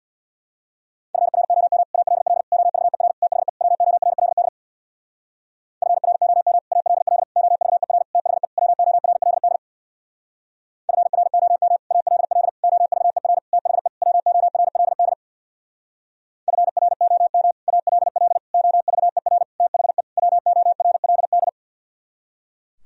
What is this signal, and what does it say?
Morse code, heard as a keyed tone, about 64 wpm, 700 Hz. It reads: FROM ALL OVER THE WORLD FROM ALL OVER THE WORLD FROM ALL OVER THE WORLD FROM ALL OVER THE WORLD